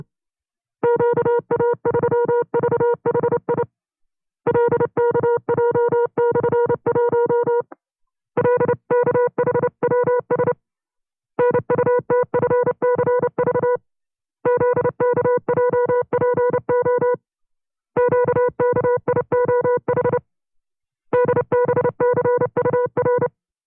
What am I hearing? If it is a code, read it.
Morse code, heard as a keyed tone, about 30 wpm, 490 Hz. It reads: QA3VHI LKJ/1 LKHWS NUTFCV ZKJPO QKIOH DBCUR
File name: FreeFiles / CW-practice / 25WPM-side2